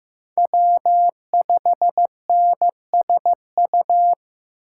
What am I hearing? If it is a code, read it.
Morse code, heard as a keyed tone, about 18 wpm, 700 Hz. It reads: W5NSU
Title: Speed Racing - USA 1x3 Call Signs 15wpm